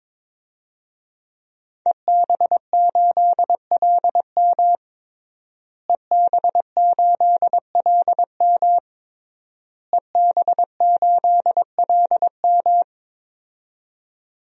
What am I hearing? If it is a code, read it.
Morse code, heard as a keyed tone, about 22 wpm, 700 Hz. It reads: EB8LM EB8LM EB8LM
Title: DXCC-100 Call Signs - 5 Characters - Repeated 3x - 22wpm